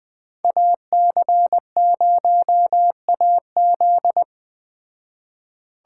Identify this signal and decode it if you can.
Morse code, heard as a keyed tone, about 20 wpm, 700 Hz. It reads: AC0AZ
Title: Speed Racing - USA 2x2 Call Signs 17wpm